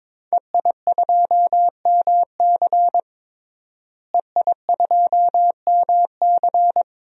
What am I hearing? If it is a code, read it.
Morse code, heard as a keyed tone, about 21 wpm, 700 Hz. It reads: EI2MC EI2MC
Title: DXCC-100 Call Signs - 5 Characters - Repeated 2x - 22wpm